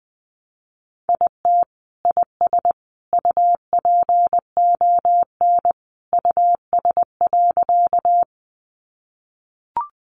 Morse code, heard as a keyed tone, about 20 wpm, 700 Hz. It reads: IT IS UPON US.
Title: Sentences from Top 1100 Words 20wpm